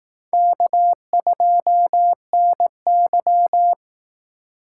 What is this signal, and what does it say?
Morse code, heard as a keyed tone, about 19 wpm, 700 Hz. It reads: K2NY
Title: Speed Racing - USA 1x2 Call Signs 15wpm